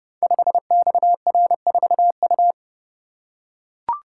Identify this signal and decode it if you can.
Morse code, heard as a keyed tone, about 28 wpm, 700 Hz. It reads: 5XR4U
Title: Code Groups - 5 Letter:Numbers - 30wpm